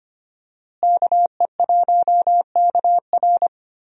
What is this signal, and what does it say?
Morse code, heard as a keyed tone, about 25 wpm, 700 Hz. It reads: KE1KR
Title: USA Call Signs - 5 Characters - Repeated 2x - 25wpm